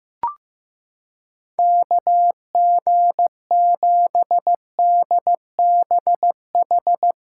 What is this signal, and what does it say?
Morse code, heard as a keyed tone, about 15 wpm, 700 Hz. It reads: KG7DBH
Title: USA Callsign DE Callsign - 6 Characters - 15wpm